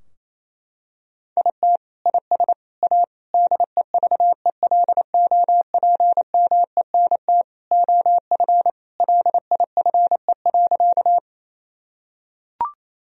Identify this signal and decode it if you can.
Morse code, heard as a keyed tone, about 28 wpm, 700 Hz. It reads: IT IS A DEVELOPMENT OF LIFE.
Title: Sentences from Top 1000 Words - Review Part 4 - 28wpm